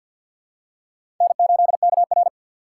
Morse code, heard as a keyed tone, about 53 wpm, 700 Hz. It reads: N8KR
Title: USA Call Signs - 4 Characters - 2x Character Spacing - 50wpm